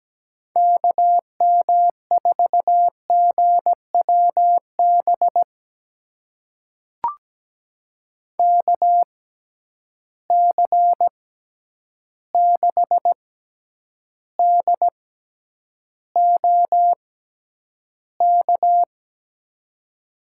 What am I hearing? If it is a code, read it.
Morse code, heard as a keyed tone, about 17 wpm, spaced as for 9 wpm, 700 Hz. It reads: KM4GWB KC6DOK
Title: USA Call Signs - 6 Characters - 6x Character Spacing - 17wpm